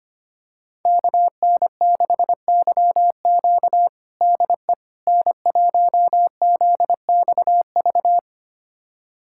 Morse code, heard as a keyed tone, about 25 wpm, 700 Hz. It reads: KN6YQ DE N1ZXV
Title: USA Callsign DE Callsign - 5 Characters - 25wpm